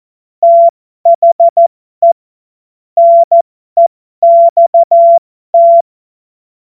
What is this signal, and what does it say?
Morse code, heard as a keyed tone, about 14 wpm, spaced as for 12 wpm, 680 Hz. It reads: THE NEXT
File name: Morsekurs / Witze